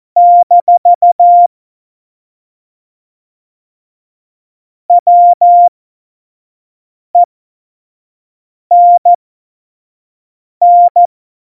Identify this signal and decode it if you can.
Morse code, heard as a keyed tone, about 14 wpm, spaced as for 5 wpm, 700 Hz. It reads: - WENN